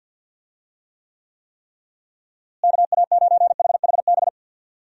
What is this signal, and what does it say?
Morse code, heard as a keyed tone, about 52 wpm, 700 Hz. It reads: KA9HHB
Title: USA Call Signs - 6 Characters - Repeated 3x - 50wpm